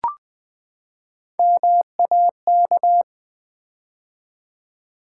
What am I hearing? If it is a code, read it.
Morse code, heard as a keyed tone, about 20 wpm, 700 Hz. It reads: MAK
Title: Build A Word - 3 Letters At A Time - 6-Letter Words - 20wpm